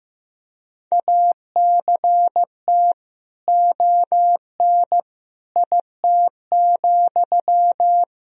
Morse code, heard as a keyed tone, about 15 wpm, 700 Hz. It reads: ACT ON IT,
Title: Sentences from Top 1000 Words - Review Part 1 - 15wpm